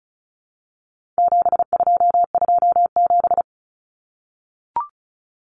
Morse code, heard as a keyed tone, about 34 wpm, 700 Hz. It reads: 7227
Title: Common Cut Numbers - 4-Digit 35wpm